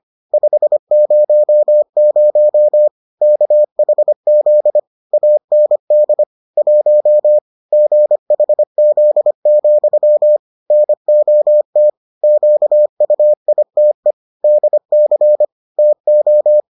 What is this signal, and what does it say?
Morse code, heard as a keyed tone, about 25 wpm, 600 Hz. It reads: 500 KHZ AND 1 GHZ, NOT QUITE DC TO